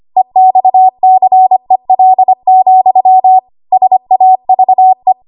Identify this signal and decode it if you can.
Morse code, heard as a keyed tone, about 25 wpm, 750 Hz. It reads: EXCEL, SAVE
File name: Archive / 25 WPM / 25WPM